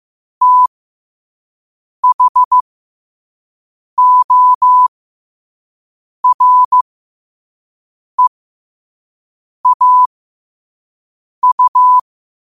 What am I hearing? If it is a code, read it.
Morse code, heard as a keyed tone, about 15 wpm, spaced as for 5 wpm, 1000 Hz. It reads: THOREAU
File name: qotdfiles / 5wpm